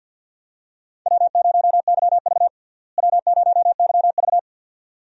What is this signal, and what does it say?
Morse code, heard as a keyed tone, about 52 wpm, 700 Hz. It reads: W0YV W0YV
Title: USA Call Signs - 4 Characters - Repeated 2x - 50wpm